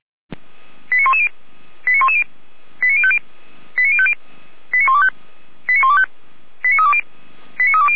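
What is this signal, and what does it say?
Five-tone selective call (ZVEI), heard as quick runs of five tones, tones of about 70 ms. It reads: 89110 89110 89050 89050 89125 89125 89320 89320
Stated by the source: ZVEI five-tone call received by SDR.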